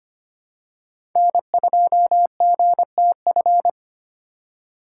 Morse code, heard as a keyed tone, about 25 wpm, 700 Hz. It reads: N2GTF
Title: USA Call Signs - 5 Characters - 4x Character Spacing - 25wpm